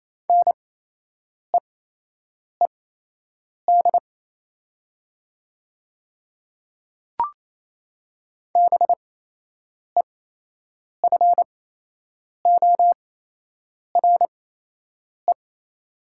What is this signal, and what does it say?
Morse code, heard as a keyed tone, about 28 wpm, spaced as for 8 wpm, 700 Hz. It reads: NEED BEFORE
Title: 8x Character Spacing - Top 200 Words - 28wpm